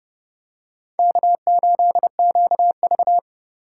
Morse code, heard as a keyed tone, about 28 wpm, 700 Hz. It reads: K8QV